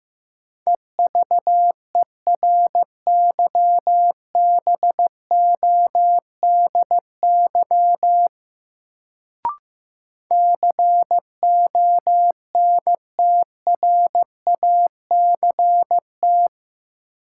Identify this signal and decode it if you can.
Morse code, heard as a keyed tone, about 15 wpm, 700 Hz. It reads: EVERYBODY CONTRACT
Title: Sentences from Top 1400 Words - New Word Drill Alt - 15wpm